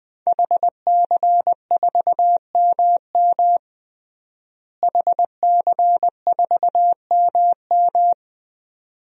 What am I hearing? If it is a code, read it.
Morse code, heard as a keyed tone, about 20 wpm, 700 Hz. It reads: HC4MM HC4MM
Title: DXCC-100 Call Signs - 5 Characters - Repeated 2x - 20wpm